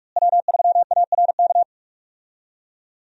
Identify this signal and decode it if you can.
Morse code, heard as a keyed tone, about 41 wpm, 700 Hz. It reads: W3ARK
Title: USA Call Signs - 5 Characters - Repeated 2x - 45wpm